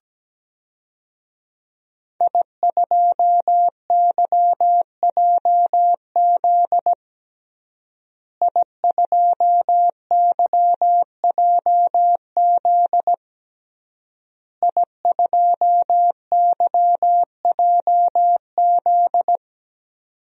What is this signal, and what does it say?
Morse code, heard as a keyed tone, about 17 wpm, 700 Hz. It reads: I2YJZ I2YJZ I2YJZ
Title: DXCC-100 Call Signs - 5 Characters - Repeated 3x - 17wpm